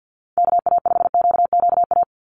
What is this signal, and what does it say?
Morse code, heard as a keyed tone, about 48 wpm, 700 Hz. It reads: KA5QQA
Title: NA SOTA Chaser Callsigns 50wpm